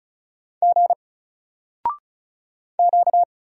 Morse code, heard as a keyed tone, about 36 wpm, 700 Hz. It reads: G Q
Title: Single Character (Letter:Number:Punctuation:Prosign) Alt 35wpm